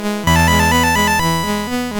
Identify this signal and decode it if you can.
There is no code.